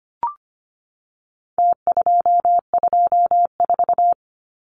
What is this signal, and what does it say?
Morse code, heard as a keyed tone, about 25 wpm, 700 Hz. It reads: T224
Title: Numbers - Contest Style - 4-Digit 25wpm